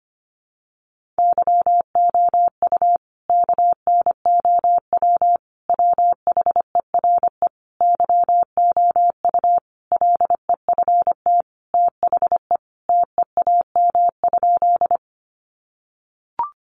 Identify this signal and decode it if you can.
Morse code, heard as a keyed tone, about 25 wpm, 700 Hz. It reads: YOU KNOW WHERE YOU LEFT THE TEAM?
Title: Sentences from Top 1500 Words 25wpm